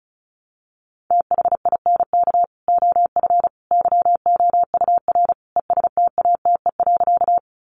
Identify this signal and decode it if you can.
Morse code, heard as a keyed tone, about 34 wpm, 700 Hz. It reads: THINK OF YOUR ESTATE.
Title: Sentences from Top 1200 Words 35wpm